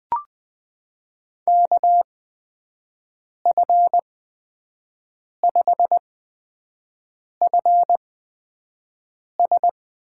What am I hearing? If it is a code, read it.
Morse code, heard as a keyed tone, about 21 wpm, spaced as for 5 wpm, 700 Hz. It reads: KF5FS
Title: USA Call Signs - 8x Character Spacing - 20wpm